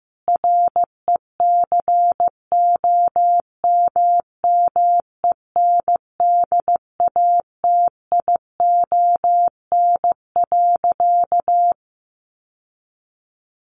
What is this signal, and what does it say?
Morse code, heard as a keyed tone, about 15 wpm, 700 Hz. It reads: RECOMMENDATION.
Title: Sentences from Top 1400 Words 15wpm